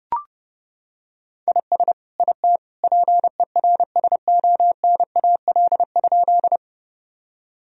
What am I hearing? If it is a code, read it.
Morse code, heard as a keyed tone, about 32 wpm, 700 Hz. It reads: IS IT PERSONAL?
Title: Sentences from Top 1000 Words - Review Part 4 - 30wpm